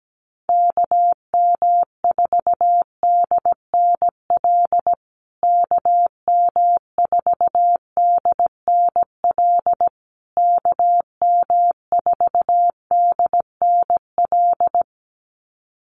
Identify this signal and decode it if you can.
Morse code, heard as a keyed tone, about 17 wpm, 700 Hz. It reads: KM4DNL KM4DNL KM4DNL
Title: USA Call Signs Repeated 3x 17wpm